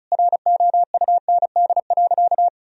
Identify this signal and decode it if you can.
Morse code, heard as a keyed tone, about 34 wpm, 700 Hz. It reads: ROUND.
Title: Sentences from Top 1000 Words - Review Part 1 - 35wpm